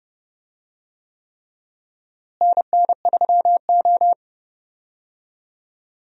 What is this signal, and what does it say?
Morse code, heard as a keyed tone, about 30 wpm, 700 Hz. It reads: NN3O
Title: USA Call Signs - 4 Characters - Repeated 3x - 30wpm